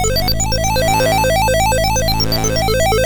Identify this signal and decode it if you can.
There is no code.